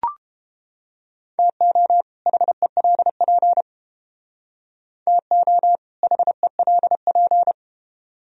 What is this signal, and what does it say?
Morse code, heard as a keyed tone, about 29 wpm, 700 Hz. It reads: TO HELP TO HELP